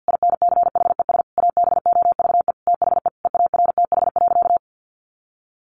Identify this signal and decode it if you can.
Morse code, heard as a keyed tone, about 49 wpm, 700 Hz. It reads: INCHES ABOVE THE EARTH.
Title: A Sound Of Thunder 50wpm